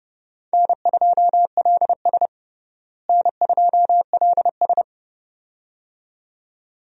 Morse code, heard as a keyed tone, about 30 wpm, 700 Hz. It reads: N2LS N2LS
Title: USA Call Signs - 4 Characters - Repeated 2x - 30wpm